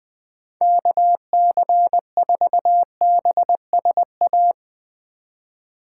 Morse code, heard as a keyed tone, about 20 wpm, 700 Hz. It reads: KC4BSA